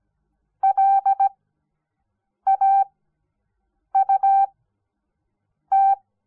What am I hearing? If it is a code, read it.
Morse code, heard as a keyed tone, about 17 wpm, spaced as for 6 wpm, 780 Hz. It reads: LAUT
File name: Onlinelehrgang / morsen / DJ4UF Morsen06